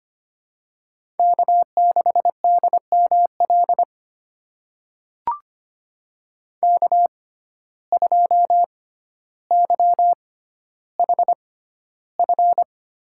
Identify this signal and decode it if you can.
Morse code, heard as a keyed tone, about 25 wpm, spaced as for 13 wpm, 700 Hz. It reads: K6DML K2YHF
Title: USA Call Signs - 5 Characters - 6x Character Spacing - 25wpm